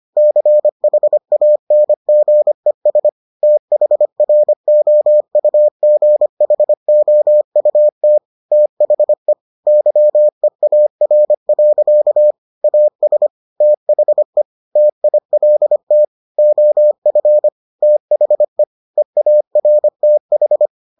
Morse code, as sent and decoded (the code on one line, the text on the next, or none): -.-. .... .- -. --. ... / - .... .-. --- ..- --. .... --- ..- - / - .... / -.-- . .- .-. .-.-.- / .- ... / - .... / - .. .-.. - / --- ..-. / - .... / . .- .-. - ....
CHANGES THROUGHOUT THE YEAR. AS THE TILT OF THE EARTH